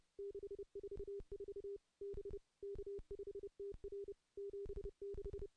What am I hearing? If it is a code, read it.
Morse code, heard as a keyed tone, about 30 wpm, 400 Hz. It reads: 644 B K5TR 76
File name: contests / CW-SS-k5tr- -0400